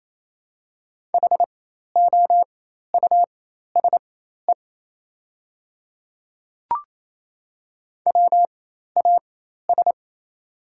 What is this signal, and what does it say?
Morse code, heard as a keyed tone, about 28 wpm, spaced as for 13 wpm, 690 Hz. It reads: HOUSE WAS
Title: 4x Character Spacing - Top 200 Words - 28wpm